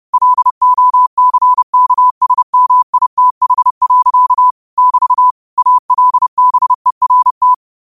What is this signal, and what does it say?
Morse code, heard as a keyed tone, about 30 wpm, 1000 Hz. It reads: LOCKSMITH. <BT> ALBERT